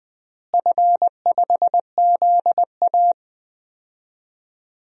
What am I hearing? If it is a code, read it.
Morse code, heard as a keyed tone, about 20 wpm, 700 Hz. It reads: F5ZA
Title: DXCC-100 Call Signs - 4 Characters - Repeated 3x - 20wpm